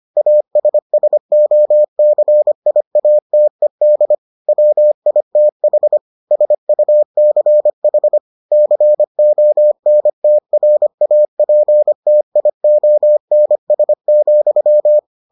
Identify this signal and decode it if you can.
Morse code, heard as a keyed tone, about 25 wpm, 600 Hz. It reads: ASSOCIATED WITH SUCH CONTRAPTIONS,